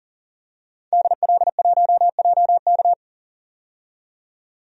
Morse code, heard as a keyed tone, about 40 wpm, 700 Hz. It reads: DL1JK
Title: DXCC-100 Call Signs - 5 Characters - 40wpm